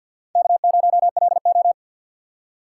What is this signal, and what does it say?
Morse code, heard as a keyed tone, about 47 wpm, 700 Hz. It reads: K0LO
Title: USA Call Signs - 4 Characters - 3x Character Spacing - 50wpm